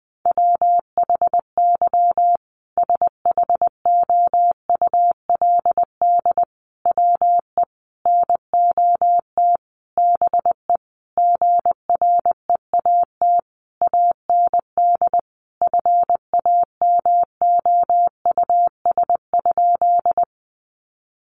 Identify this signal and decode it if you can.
Morse code, heard as a keyed tone, about 20 wpm, 700 Hz. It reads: WHY SHOULD WE NOT BE GREAT AND FAMOUS?
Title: Sentences from Top 1500 Words 20wpm